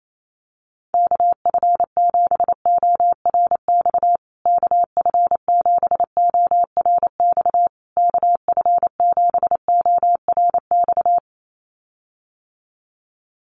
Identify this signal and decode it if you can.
Morse code, heard as a keyed tone, about 28 wpm, 700 Hz. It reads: KF7ORX KF7ORX KF7ORX
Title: USA Call Signs Repeated 3x 28wpm